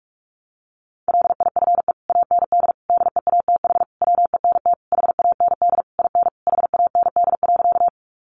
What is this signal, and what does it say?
Morse code, heard as a keyed tone, about 46 wpm, 700 Hz. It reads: LIFE AND DEATH WENT HAND IN HAND.
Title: Sentences from Top 1100 Words 45wpm